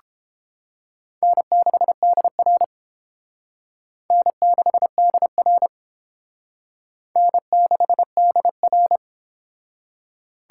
Morse code, heard as a keyed tone, about 29 wpm, 700 Hz. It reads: N6DR N6DR N6DR